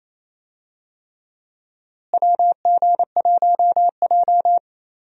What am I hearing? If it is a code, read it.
Morse code, heard as a keyed tone, about 28 wpm, 700 Hz. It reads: WG1J
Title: USA Call Signs - 4 Characters - Repeated 3x - 28wpm